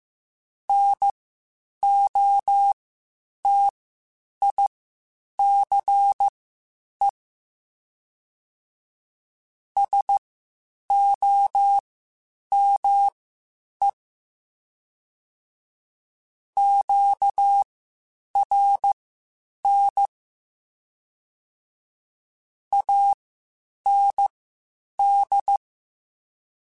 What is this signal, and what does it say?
Morse code, heard as a keyed tone, about 15 wpm, spaced as for 8 wpm, 800 Hz. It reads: NOTICE SOME QRN AND
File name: QSOs / QSOs-mp3s / WD0BGP 1@10